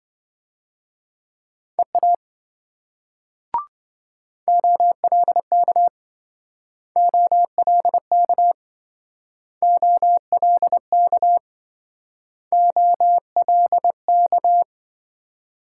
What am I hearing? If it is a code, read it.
Morse code, heard as a keyed tone, about 27 wpm, 700 Hz. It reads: EA OLK OLK OLK OLK